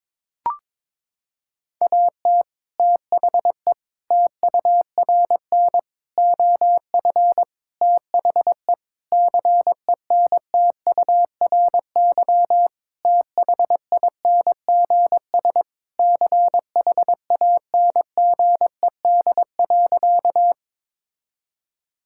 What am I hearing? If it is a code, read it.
Morse code, heard as a keyed tone, about 22 wpm, 700 Hz. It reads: AT THE TURN OF THE CENTURY THINGS CHANGED.
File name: practice / 22wpm / Sentences with 4-Word Phrases - 22wpm